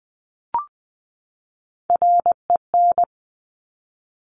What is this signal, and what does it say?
Morse code, heard as a keyed tone, about 20 wpm, 700 Hz. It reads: REN
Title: Top Three Letter Combinations in Words 20wpm